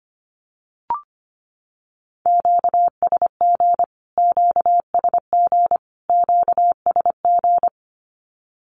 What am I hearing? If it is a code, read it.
Morse code, heard as a keyed tone, about 25 wpm, 700 Hz. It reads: QSG QSG QSG
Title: Q-Codes Repeated 3x 25wpm